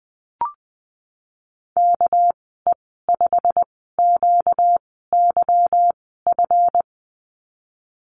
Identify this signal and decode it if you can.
Morse code, heard as a keyed tone, about 19 wpm, spaced as for 14 wpm, 700 Hz. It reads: KE5QYF
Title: 238 Lesson 38g - Letter q - Call Signs 20wpm